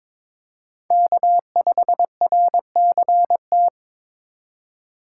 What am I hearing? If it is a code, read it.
Morse code, heard as a keyed tone, about 26 wpm, 700 Hz. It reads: K5RCT
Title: Speed Racing - USA Call Signs 22wpm